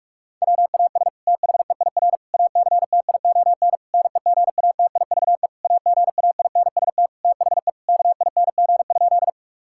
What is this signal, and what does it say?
Morse code, heard as a keyed tone, about 46 wpm, 700 Hz. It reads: WAS THEIR ACTION NEGATIVE AGAINST THE KING?
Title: Sentences from Top 1000 Words - Review Part 6 - 45wpm